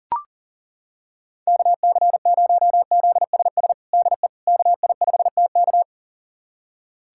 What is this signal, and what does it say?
Morse code, heard as a keyed tone, about 37 wpm, 700 Hz. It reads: KC0ZSS DE KI5TK
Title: USA Callsign DE Callsign 40wpm